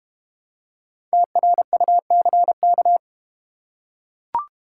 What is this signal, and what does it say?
Morse code, heard as a keyed tone, about 32 wpm, 700 Hz. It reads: TRUCK